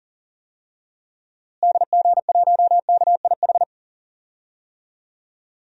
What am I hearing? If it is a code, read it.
Morse code, heard as a keyed tone, about 37 wpm, 700 Hz. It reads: DG1KIH
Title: DXCC-100 Call Signs - 6 Characters - Repeated 3x - 40wpm